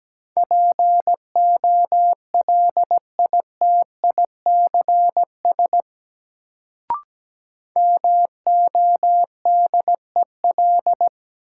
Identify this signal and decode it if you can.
Morse code, heard as a keyed tone, about 17 wpm, 700 Hz. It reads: POLITICS MODEL
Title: Sentences from Top 1000 Words - New Word Drill - 17wpm